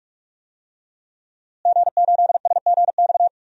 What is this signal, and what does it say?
Morse code, heard as a keyed tone, about 46 wpm, 700 Hz. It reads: G8SGX
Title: DXCC-100 Call Signs - 5 Characters - Repeated 3x - 45wpm